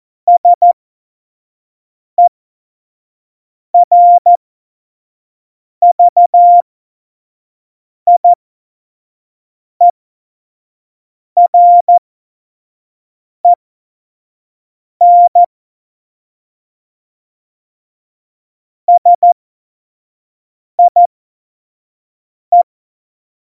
Morse code, heard as a keyed tone, about 14 wpm, spaced as for 5 wpm, 700 Hz. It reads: SERVIEREN SIE
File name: Morsekurs / Witze / Witz 12. November 2019, 25 F70